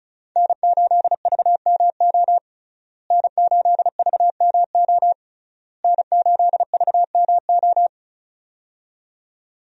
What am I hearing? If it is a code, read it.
Morse code, heard as a keyed tone, about 36 wpm, 700 Hz. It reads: N8VMO N8VMO N8VMO
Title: USA Call Signs - 5 Characters - Repeated 3x - 35wpm